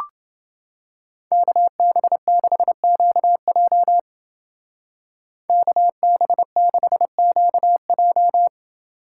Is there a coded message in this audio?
Yes: Morse code from 1.3 to 8.5 s, heard as a keyed tone, about 28 wpm, 700 Hz. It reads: KB6QJ KB6QJ